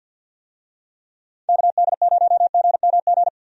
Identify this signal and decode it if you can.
Morse code, heard as a keyed tone, about 50 wpm, 700 Hz. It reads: KD0GMG
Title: USA Call Signs - 6 Characters - 50wpm